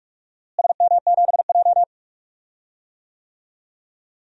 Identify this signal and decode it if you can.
Morse code, heard as a keyed tone, about 42 wpm, 700 Hz. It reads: SM7J
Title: DXCC-100 Call Signs - 4 Characters - Repeated 3x - 45wpm